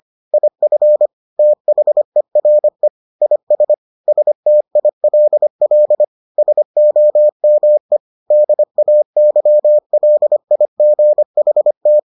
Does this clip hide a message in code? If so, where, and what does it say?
Yes: Morse code from 0.3 to 12.0 s, heard as a keyed tone, about 25 wpm, 600 Hz. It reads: IF THERE IS STILL SOME DAYLIGHT